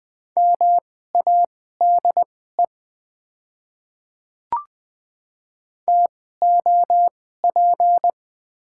Morse code, heard as a keyed tone, about 20 wpm, spaced as for 14 wpm, 690 Hz. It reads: MADE TOP
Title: 2x Character Spacing - Top 500 Words - 20wpm